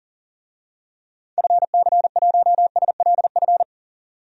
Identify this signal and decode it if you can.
Morse code, heard as a keyed tone, about 37 wpm, 700 Hz. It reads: FC1SLF
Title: DXCC-100 Call Signs - 6 Characters - 40wpm